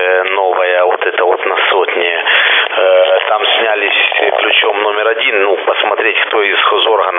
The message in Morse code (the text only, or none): none